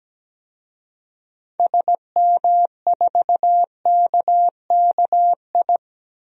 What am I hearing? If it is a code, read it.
Morse code, heard as a keyed tone, about 17 wpm, 700 Hz. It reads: SM4KKI
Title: DXCC-100 Call Signs - 6 Characters - 17wpm